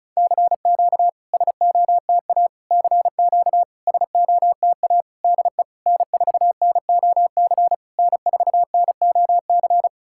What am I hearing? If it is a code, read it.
Morse code, heard as a keyed tone, about 36 wpm, 700 Hz. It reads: CQ SOTA CQ SOTA DE N4NOC N4NOC
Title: SOTA QSO Practice 35wpm